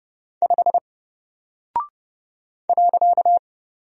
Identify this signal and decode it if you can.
Morse code, heard as a keyed tone, about 29 wpm, 700 Hz. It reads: 5 .
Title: Single Character (Letter:Number:Punctuation:Prosign) Alt 30wpm